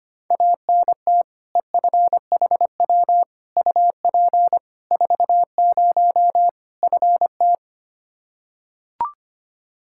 Morse code, heard as a keyed tone, about 25 wpm, 700 Hz. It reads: ANT EFHW UP 40 FT
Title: K1USN SST QSO Practice 25wpm